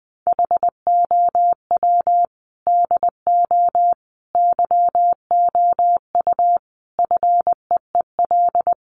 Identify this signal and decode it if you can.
Morse code, heard as a keyed tone, about 20 wpm, 700 Hz. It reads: HOW DO YOU FEEL
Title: Sets of 5 Words 20wpm